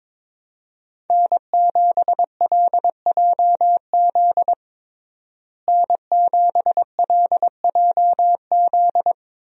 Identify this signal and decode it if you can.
Morse code, heard as a keyed tone, about 22 wpm, 700 Hz. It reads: N7LJZ N7LJZ
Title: USA Call Signs - 5 Characters - Repeated 2x - 22wpm